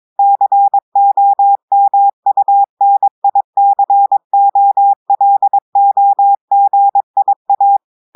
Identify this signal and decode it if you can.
Morse code, heard as a keyed tone, about 22 wpm, 800 Hz. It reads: COMUNICOLOGIA
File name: audios / portugais / portugais texte en clair 1 vitesse 22 mots minutes